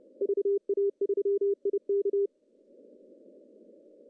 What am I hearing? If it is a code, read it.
Morse code, heard as a keyed tone, about 34 wpm, 400 Hz. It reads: VA3IK
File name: CW / cqww-cw-k5tr